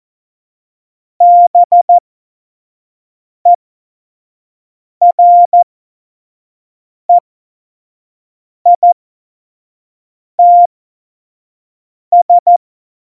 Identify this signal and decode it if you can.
Morse code, heard as a keyed tone, about 14 wpm, spaced as for 5 wpm, 700 Hz. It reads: BEREITS
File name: Morsekurs / Witze / Witz 15. September 2020, 25 F70